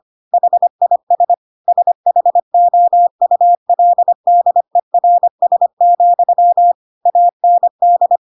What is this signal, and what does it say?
Morse code, heard as a keyed tone, about 25 wpm, 700 Hz. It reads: HIS SHOULDERS, AND